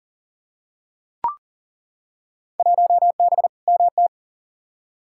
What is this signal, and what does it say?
Morse code, heard as a keyed tone, about 40 wpm, 700 Hz. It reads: 1B MT
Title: Field Day Exchanges - 40wpm